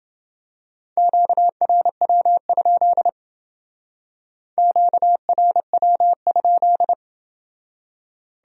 Morse code, heard as a keyed tone, about 28 wpm, 700 Hz. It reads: QRW? QRW?